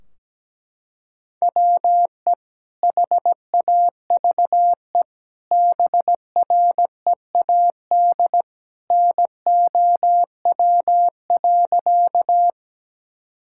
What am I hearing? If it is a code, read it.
Morse code, heard as a keyed tone, about 17 wpm, 700 Hz. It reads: WE HAVE BREAD NOW.
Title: Sentences from Top 1600 Words 17wpm